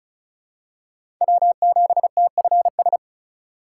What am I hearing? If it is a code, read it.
Morse code, heard as a keyed tone, about 36 wpm, 700 Hz. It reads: W7TFS
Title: USA Call Signs - 2x Character Spacing - 35wpm